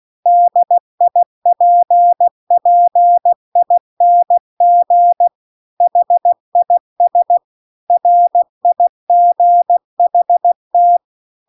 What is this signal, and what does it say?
Morse code, heard as a keyed tone, about 16 wpm, 700 Hz. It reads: DIPPING HIS RIGHT
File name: CW-The-Big-Twoheaded-River-2-16WPM